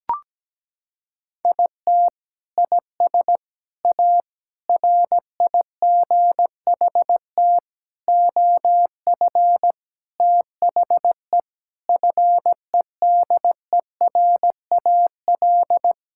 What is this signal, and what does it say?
Morse code, heard as a keyed tone, about 17 wpm, 700 Hz. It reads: IT IS A RIGHT OF THE FEDERAL
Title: Sentences from Top 1000 Words - Review Part 5 - 17wpm